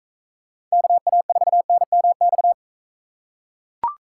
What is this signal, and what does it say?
Morse code, heard as a keyed tone, about 42 wpm, 700 Hz. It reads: KA4NMX